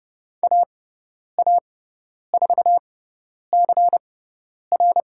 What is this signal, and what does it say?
Morse code, heard as a keyed tone, about 30 wpm, spaced as for 10 wpm, 700 Hz. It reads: AA4CR
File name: Lesson 16g - Letter c - Call Signs 30wpm